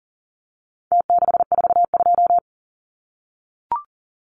Morse code, heard as a keyed tone, about 40 wpm, 700 Hz. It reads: T642
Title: Numbers - Contest Style - 4-Digit 40wpm